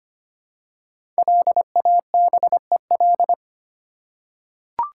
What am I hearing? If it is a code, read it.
Morse code, heard as a keyed tone, about 25 wpm, 700 Hz. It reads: LABEL
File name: 5 Letter Words - First 3 Letters Given - 25wpm